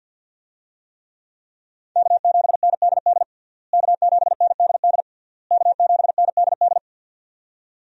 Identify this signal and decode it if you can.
Morse code, heard as a keyed tone, about 48 wpm, 700 Hz. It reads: K7NDD K7NDD K7NDD
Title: USA Call Signs - 5 Characters - Repeated 3x - 50wpm